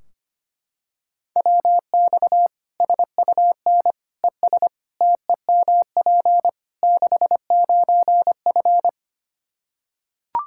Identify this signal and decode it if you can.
Morse code, heard as a keyed tone, about 25 wpm, 700 Hz. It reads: WX SUN ES TEMP 69F
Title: K1USN SST QSO Practice 25wpm